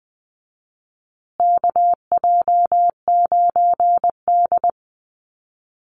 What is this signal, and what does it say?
Morse code, heard as a keyed tone, about 20 wpm, 700 Hz. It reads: KJ9D
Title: USA Call Signs Repeated 3x 20wpm